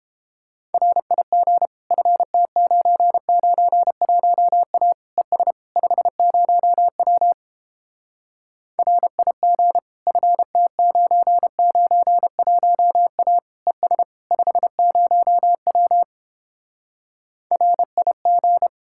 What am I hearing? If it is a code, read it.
Morse code, heard as a keyed tone, about 31 wpm, 700 Hz. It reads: RIG FT991A ES 50W RIG FT991A ES 50W RIG